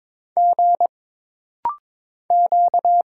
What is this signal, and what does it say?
Morse code, heard as a keyed tone, about 22 wpm, 700 Hz. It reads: G Q
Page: This rapid-fire timing is geared to push your ICR (Instant Character Recognition) skills.